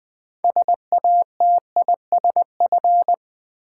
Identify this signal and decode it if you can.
Morse code, heard as a keyed tone, about 20 wpm, 700 Hz. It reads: SATISF
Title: Build A Word - 6 Letters At A Time - 12-Letter Words - 20wpm